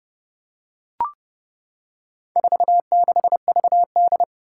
Morse code, heard as a keyed tone, about 30 wpm, 700 Hz. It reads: 46VD
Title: Four Challenging Letter-Number Combinations 30wpm